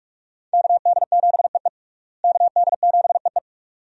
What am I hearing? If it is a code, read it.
Morse code, heard as a keyed tone, about 46 wpm, 700 Hz. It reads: KD7EE KD7EE
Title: USA Call Signs - 5 Characters - Repeated 2x - 45wpm